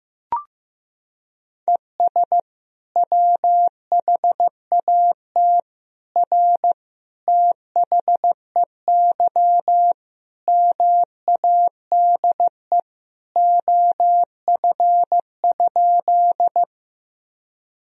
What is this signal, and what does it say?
Morse code, heard as a keyed tone, about 16 wpm, 700 Hz. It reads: ES WHAT R THEY MADE OF?
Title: Sentences from Top 200 Words and Abbreviations 15wpm